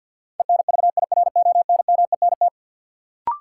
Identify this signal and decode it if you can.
Morse code, heard as a keyed tone, about 51 wpm, 700 Hz. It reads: ENVIRONMENT
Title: Sentences from Top 1000 Words - Review Part 5 - New Word Drill - 50wpm